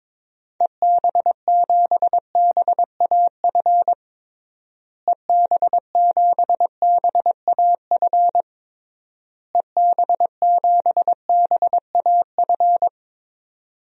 Morse code, heard as a keyed tone, about 21 wpm, 700 Hz. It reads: EB7BAF EB7BAF EB7BAF
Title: DXCC-100 Call Signs - 6 Characters - Repeated 3x - 22wpm